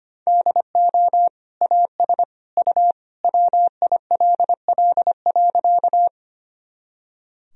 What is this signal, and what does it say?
Morse code, heard as a keyed tone, about 25 wpm, 700 Hz. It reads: DO AS U WILL.
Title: Sentences from Top 100 Words and Abbreviations 25wpm